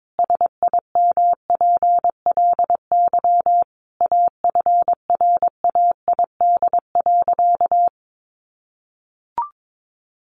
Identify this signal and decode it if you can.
Morse code, heard as a keyed tone, about 22 wpm, 700 Hz. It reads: SIMPLY AFRAID.
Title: Sentences from Top 1500 Words 22wpm